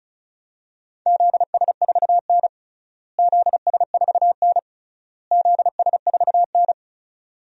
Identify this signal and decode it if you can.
Morse code, heard as a keyed tone, about 34 wpm, 700 Hz. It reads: ZS4N ZS4N ZS4N
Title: DXCC-100 Call Signs - 4 Characters - Repeated 3x - 35wpm